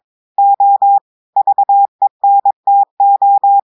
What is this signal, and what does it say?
Morse code, heard as a keyed tone, about 22 wpm, 800 Hz. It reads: O VENTO
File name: audios / portugais / portugais texte en clair 1 vitesse 22 mots minutes